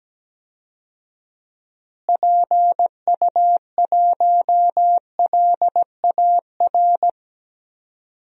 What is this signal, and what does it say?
Morse code, heard as a keyed tone, about 17 wpm, 700 Hz. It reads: PU1LAR
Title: DXCC-100 Call Signs - 6 Characters - Repeated 3x - 17wpm